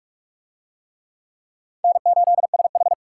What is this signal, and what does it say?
Morse code, heard as a keyed tone, about 44 wpm, 700 Hz. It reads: N8SH